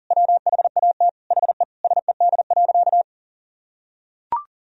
Morse code, heard as a keyed tone, about 40 wpm, 700 Hz. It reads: WHAT HE SED.
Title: Sentences from Top 400 Words and Abbreviations 40wpm